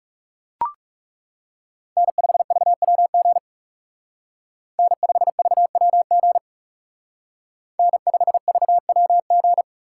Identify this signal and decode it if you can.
Morse code, heard as a keyed tone, about 34 wpm, 700 Hz. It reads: N5VWG N5VWG N5VWG
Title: Speed Racing - USA Call Signs 30wpm